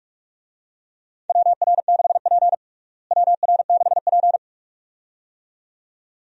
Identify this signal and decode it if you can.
Morse code, heard as a keyed tone, about 46 wpm, 700 Hz. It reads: WR6P WR6P
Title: USA Call Signs - 4 Characters - Repeated 2x - 45wpm